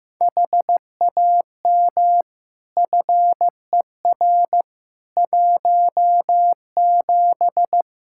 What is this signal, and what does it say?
Morse code, heard as a keyed tone, about 15 wpm, 700 Hz. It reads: HAM FER 17
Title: Practice QSO - Exchange 3 - More Info - 15wpm